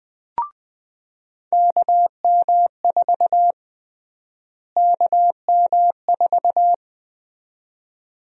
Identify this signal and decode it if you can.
Morse code, heard as a keyed tone, about 21 wpm, 700 Hz. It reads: KM4 KM4
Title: USA Call Sign Prefixes - Repeated 2x - 20wpm